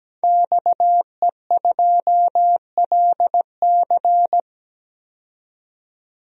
Morse code, heard as a keyed tone, about 17 wpm, 700 Hz. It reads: XE2LC
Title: DXCC-100 Call Signs - 5 Characters - 17wpm